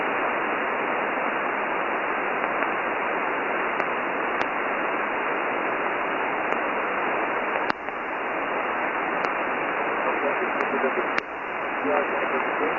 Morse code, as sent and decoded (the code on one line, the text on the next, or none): none